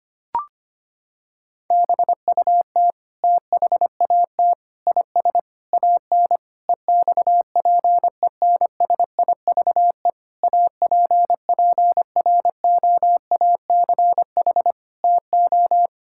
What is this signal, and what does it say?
Morse code, heard as a keyed tone, about 25 wpm, 700 Hz. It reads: BUT THAT IS AN EXPENSIVE APPROACH TO
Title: Sentences from Top 1000 Words - Review Part 6 - 25wpm